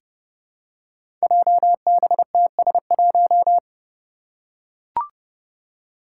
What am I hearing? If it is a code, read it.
Morse code, heard as a keyed tone, about 30 wpm, 700 Hz. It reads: JBTS1
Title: Five Challenging Letter-Number Combinations 30wpm